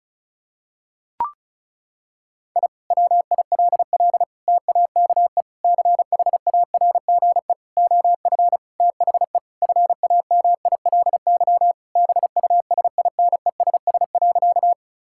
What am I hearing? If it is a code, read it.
Morse code, heard as a keyed tone, about 36 wpm, 700 Hz. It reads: I WILL TAKE CHARGE OF THE FAMILY BUSINESS.
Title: Sentences from Top 1000 Words - Review Part 6 - 35wpm